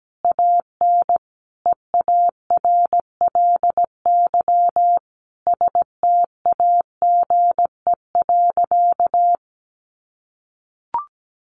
Morse code, heard as a keyed tone, about 17 wpm, 700 Hz. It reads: AN EARLY STAGE.
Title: Sentences from Top 1200 Words 17wpm